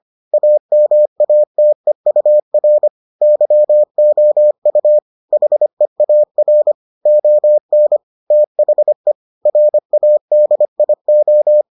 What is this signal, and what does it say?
Morse code, heard as a keyed tone, about 25 wpm, 600 Hz. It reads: AMATEUR YOU HEAR ON THE RADIO